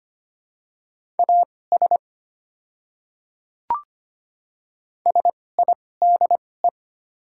Morse code, heard as a keyed tone, about 25 wpm, spaced as for 18 wpm, 700 Hz. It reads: AS SIDE
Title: 2x Character Spacing - Top 200 Words - 25wpm